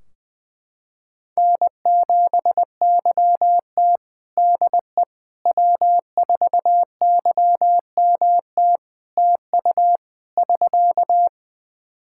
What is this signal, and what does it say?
Morse code, heard as a keyed tone, about 20 wpm, 700 Hz. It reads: N7YT DE W4YMT TU <SK>
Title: K1USN SST QSO Practice 20wpm